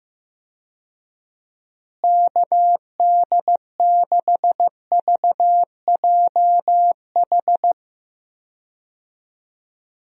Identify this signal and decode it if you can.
Morse code, heard as a keyed tone, about 16 wpm, 700 Hz. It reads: KD6VJH
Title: USA Call Signs - 6 Characters - Repeated 2x - 15wpm